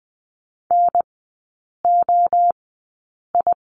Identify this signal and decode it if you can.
Morse code, heard as a keyed tone, about 20 wpm, 700 Hz. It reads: N O I
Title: 029 Lesson 6e - Letter i - Three Characters 20wpm